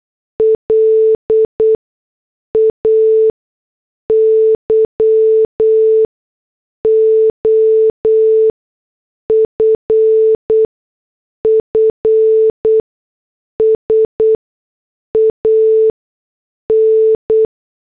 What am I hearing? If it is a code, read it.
Morse code, heard as a keyed tone, about 8 wpm, spaced as for 6 wpm, 440 Hz. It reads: LAYOFFS AN